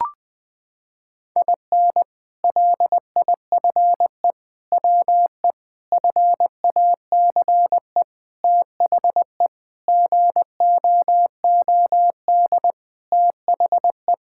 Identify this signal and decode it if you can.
Morse code, heard as a keyed tone, about 20 wpm, 700 Hz. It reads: IN LIFE WE FACE THE GOOD THE
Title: Sentences with 7-Word Phrases - 20wpm